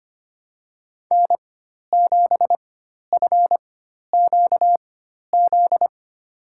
Morse code, heard as a keyed tone, about 25 wpm, spaced as for 12 wpm, 700 Hz. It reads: N7FQZ
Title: USA Call Signs - 5 Characters - 4x Character Spacing - 25wpm